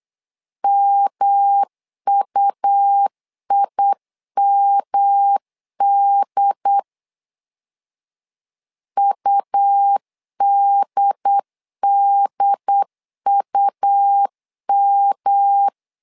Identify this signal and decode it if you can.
Morse code, heard as a keyed tone, about 8 wpm, 800 Hz. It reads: MUIMD UDDUM
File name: LEZIONI / 3 MWRO / 3 M 8wpm 2300